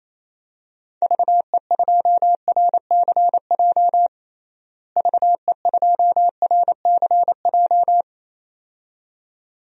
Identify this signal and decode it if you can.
Morse code, heard as a keyed tone, about 28 wpm, 700 Hz. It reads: VE2RCJ VE2RCJ
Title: DXCC-100 Call Signs - 6 Characters - Repeated 2x - 28wpm